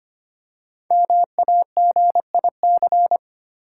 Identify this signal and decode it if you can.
Morse code, heard as a keyed tone, about 26 wpm, 700 Hz. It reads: MAGIC
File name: 5 Letter Words - First 3 Letters Given - 25wpm